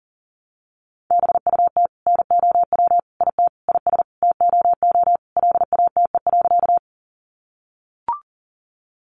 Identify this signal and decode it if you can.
Morse code, heard as a keyed tone, about 40 wpm, 700 Hz. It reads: BUT NOW IT IS TOO LATE.
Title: Sentences from Top 300 Words 40wpm